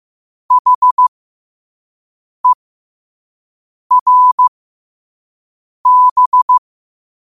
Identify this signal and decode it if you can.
Morse code, heard as a keyed tone, about 15 wpm, spaced as for 5 wpm, 1000 Hz. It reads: HERB
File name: qotdfiles / 5wpm